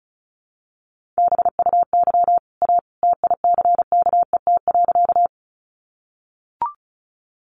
Morse code, heard as a keyed tone, about 34 wpm, 700 Hz. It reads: BUY A TICKET.
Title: Sentences from Top 1400 Words 35wpm